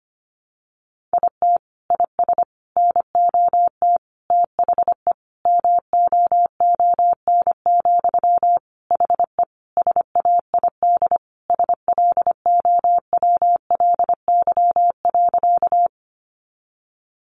Morse code, heard as a keyed tone, about 25 wpm, 700 Hz. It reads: IT IS NOT THE MOON, HE SAID SLOWLY.
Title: Sentences from Top 1100 Words 25wpm